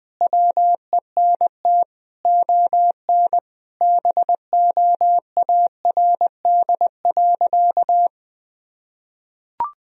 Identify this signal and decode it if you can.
Morse code, heard as a keyed tone, about 20 wpm, 700 Hz. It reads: WENT ON BOARD.
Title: Sentences from Top 1000 Words - Review Part 4 - 20wpm